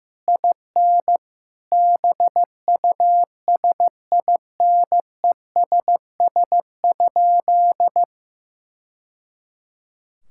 Morse code, heard as a keyed tone, about 15 wpm, 700 Hz. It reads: IN BUSINESS?
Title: Sentences from Top 1000 Words - Review Part 5 - 15wpm